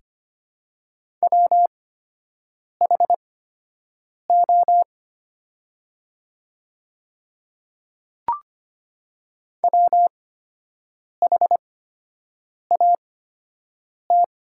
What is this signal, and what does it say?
Morse code, heard as a keyed tone, about 25 wpm, spaced as for 7 wpm, 700 Hz. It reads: WHO WHAT
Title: 8x Character Spacing - Top 100 Words - 25wpm